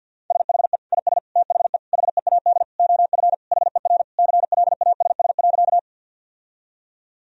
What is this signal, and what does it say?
Morse code, heard as a keyed tone, about 51 wpm, 700 Hz. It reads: SHE IS THE HEAD OF HER CLASS.
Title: Sentences from Top 1000 Words - Review Part 2 - 50wpm